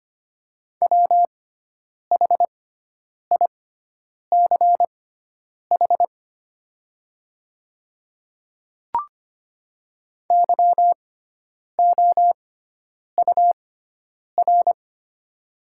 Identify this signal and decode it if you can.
Morse code, heard as a keyed tone, about 25 wpm, spaced as for 9 wpm, 700 Hz. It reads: WHICH YOUR
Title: 6x Character Spacing - Top 100 Words - 25wpm